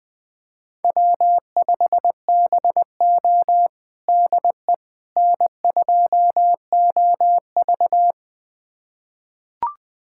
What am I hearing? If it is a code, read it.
Morse code, heard as a keyed tone, about 20 wpm, 700 Hz. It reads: W5BO DE N2OV
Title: USA Callsign DE Callsign - 4 Characters - 20wpm